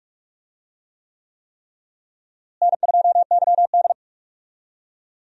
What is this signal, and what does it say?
Morse code, heard as a keyed tone, about 46 wpm, 700 Hz. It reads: N2YD
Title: USA Call Signs - 4 Characters - Repeated 3x - 45wpm